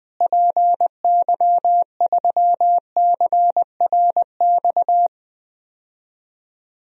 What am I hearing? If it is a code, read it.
Morse code, heard as a keyed tone, about 21 wpm, 700 Hz. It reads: PY3CRX
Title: DXCC-100 Call Signs - 6 Characters - Repeated 3x - 20wpm